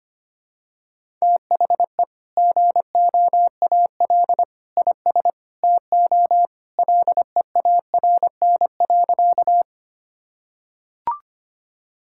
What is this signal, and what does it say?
Morse code, heard as a keyed tone, about 25 wpm, 700 Hz. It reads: THE GOAL IS TO LEARN.
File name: practice / 25wpm / Sentences with 2-Word Phrases - 25wpm